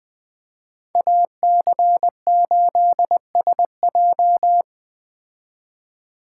Morse code, heard as a keyed tone, about 20 wpm, 700 Hz. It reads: AC8SJ